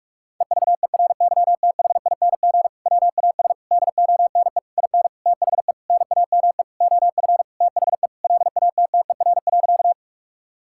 Morse code, heard as a keyed tone, about 46 wpm, 700 Hz. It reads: EVERYTHING WAS DONE IN THE NAME OF THE LATTER.
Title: Sentences from Top 1600 Words 45wpm